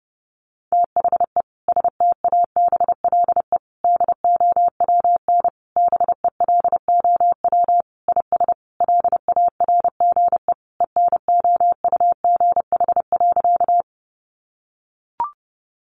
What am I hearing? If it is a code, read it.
Morse code, heard as a keyed tone, about 30 wpm, 700 Hz. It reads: THE STABLE DOWN BELOW IS LARGE ENOUGH.
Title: Sentences from Top 1500 Words 30wpm